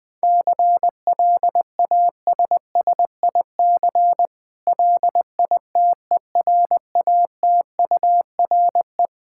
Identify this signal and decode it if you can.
Morse code, heard as a keyed tone, about 20 wpm, 700 Hz. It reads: CLASSIC LITERATURE
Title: Difficult Words with Hints - 20wpm